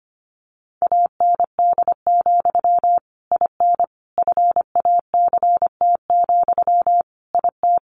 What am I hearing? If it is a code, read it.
Morse code, heard as a keyed tone, about 25 wpm, 700 Hz. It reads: AND, IN FACT, IT
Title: Sentences from Top 1200 Words 25wpm